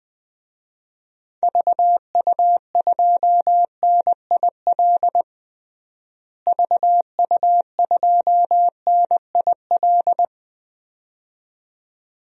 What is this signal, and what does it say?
Morse code, heard as a keyed tone, about 21 wpm, 700 Hz. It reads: VU2NIL VU2NIL
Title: DXCC-100 Call Signs - 6 Characters - Repeated 2x - 20wpm